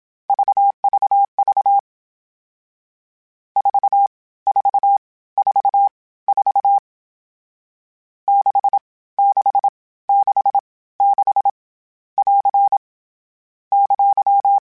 Morse code, heard as a keyed tone, about 27 wpm, spaced as for 17 wpm, 800 Hz. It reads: VVV 4444 6666 <AR> !
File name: Short Runs / Short Runs-mp3s / Staccato @25